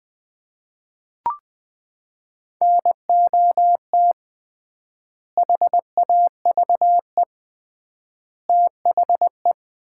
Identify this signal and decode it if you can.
Morse code, heard as a keyed tone, about 20 wpm, 700 Hz. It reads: NOT HAVE THE
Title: 3x Word-Spacing - Sets of 3 Words - 20wpm